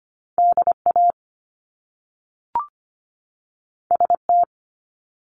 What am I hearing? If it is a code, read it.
Morse code, heard as a keyed tone, about 25 wpm, 700 Hz. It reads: DA ST
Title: Top Two Letter Combinations in Words 25wpm